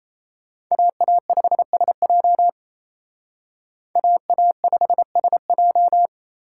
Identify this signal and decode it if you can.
Morse code, heard as a keyed tone, about 35 wpm, 700 Hz. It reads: AA5SJ AA5SJ